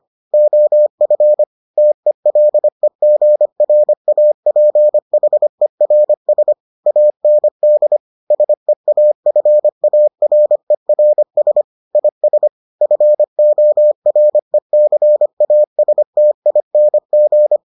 Morse code, heard as a keyed tone, about 25 wpm, 600 Hz. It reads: OF TELEGRAPHERS AND SEAFARERS IS FORECASTING